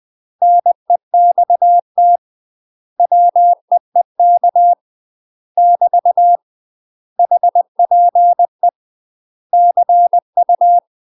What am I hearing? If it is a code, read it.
Morse code, heard as a keyed tone, about 20 wpm, 700 Hz. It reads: NEXT WEEK <BT> HPE CU
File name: UK MASTER QSOs TRAINING / 20WPM / UK MASTER QSOs 0060